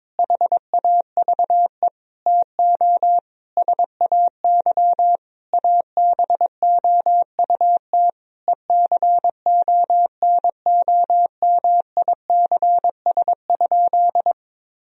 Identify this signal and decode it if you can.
Morse code, heard as a keyed tone, about 22 wpm, 700 Hz. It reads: HAVE TO SAY ABOUT ECONOMICS?
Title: Sentences from Top 1000 Words - Review Part 3 - 22wpm